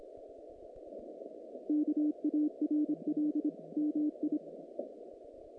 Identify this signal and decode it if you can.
Morse code, heard as a keyed tone, about 25 wpm, 300 Hz. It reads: KARL MI
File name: contests / naqp-cw-k5tr-op-k5pi- -2200